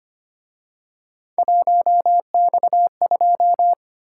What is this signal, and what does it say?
Morse code, heard as a keyed tone, about 29 wpm, 700 Hz. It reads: 1X2
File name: Speed-Racing - Three Letter-Number Combinations - 25wpm